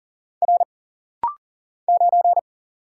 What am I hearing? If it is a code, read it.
Morse code, heard as a keyed tone, about 39 wpm, 700 Hz. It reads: R 9
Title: Single Letter-Number - Rapid-Fire 40wpm